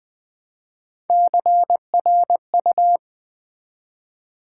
Morse code, heard as a keyed tone, about 20 wpm, 700 Hz. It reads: CRU
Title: Build A Word - 1 Letter At A Time - 5-Letter Words - 20wpm